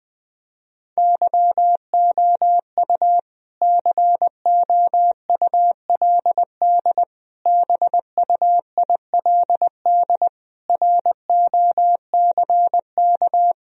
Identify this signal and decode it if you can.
Morse code, heard as a keyed tone, about 20 wpm, 700 Hz. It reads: YOU COULD BUILD ROCK
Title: Sentences from Top 1000 Words - Review Part 2 - 20wpm